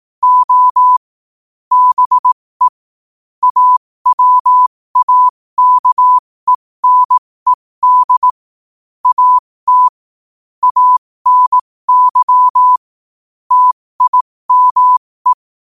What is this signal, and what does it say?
Morse code, heard as a keyed tone, about 18 wpm, spaced as for 15 wpm, 1000 Hz. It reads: O BE AWAKENED AT ANY TIME
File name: 15wpm